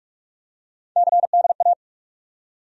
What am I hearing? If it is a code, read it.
Morse code, heard as a keyed tone, about 48 wpm, 700 Hz. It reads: CDA
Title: USA Call Sign Postfixes - 3 Letters 45wpm